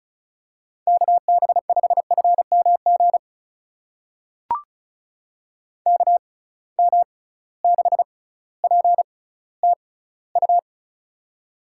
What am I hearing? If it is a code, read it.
Morse code, heard as a keyed tone, about 36 wpm, spaced as for 18 wpm, 700 Hz. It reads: KB5FMG KM6PTU
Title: USA Call Signs - 6 Characters - 6x Character Spacing - 35wpm